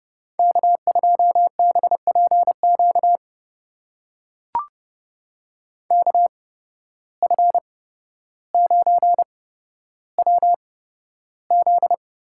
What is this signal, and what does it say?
Morse code, heard as a keyed tone, about 31 wpm, spaced as for 13 wpm, 700 Hz. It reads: K2BPQ KF9WZ
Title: USA Call Signs - 5 Characters - 8x Character Spacing - 30wpm